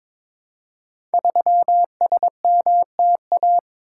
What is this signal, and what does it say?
Morse code, heard as a keyed tone, about 22 wpm, 700 Hz. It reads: 3SMTA
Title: Code Groups - 5 Letter:Numbers - 22wpm